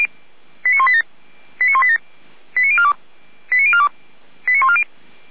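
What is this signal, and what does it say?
Five-tone selective call (ZVEI), heard as quick runs of five tones, tones of about 70 ms. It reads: 89187 89187 89042 89042 89140
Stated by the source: ZVEI five-tone selective call received on an SDR.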